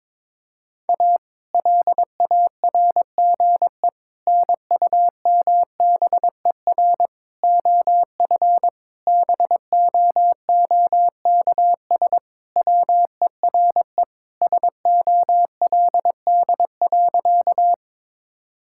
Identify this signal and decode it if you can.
Morse code, heard as a keyed tone, about 22 wpm, 700 Hz. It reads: A LARGE NUMBER OF BOOKS WERE SOLD.
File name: practice / 22wpm / Sentences with 4-Word Phrases - 22wpm